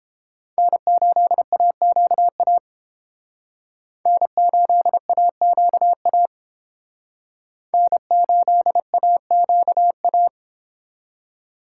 Morse code, heard as a keyed tone, about 30 wpm, 700 Hz. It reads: N8AQA N8AQA N8AQA